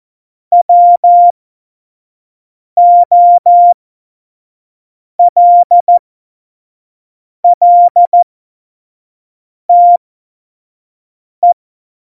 Morse code, heard as a keyed tone, about 14 wpm, spaced as for 5 wpm, 700 Hz. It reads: WOLLTE
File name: Morsekurs / Witze / Witz 07 Juli 2020, 25 F70